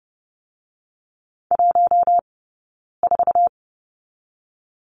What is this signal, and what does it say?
Morse code, heard as a keyed tone, about 30 wpm, 700 Hz. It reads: 1 4
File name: course / 30:10wpm / 086 Lesson 15d - Number 5 - Two Characters 30wpm